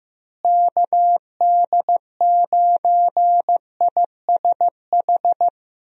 Morse code, heard as a keyed tone, about 15 wpm, 700 Hz. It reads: KD9ISH
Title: USA Call Signs - 6 Characters - Repeated 3x - 15wpm